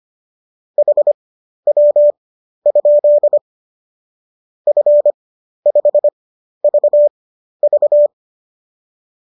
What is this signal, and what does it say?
Morse code, heard as a keyed tone, about 25 wpm, spaced as for 12 wpm, 590 Hz. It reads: HW? F5VV
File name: qso009 12